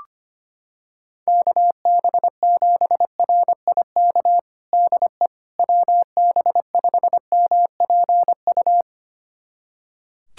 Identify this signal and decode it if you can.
Morse code, heard as a keyed tone, about 25 wpm, 700 Hz. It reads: KB7RIK DE WB5MPU
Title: USA Callsign DE Callsign - 6 Characters - 25wpm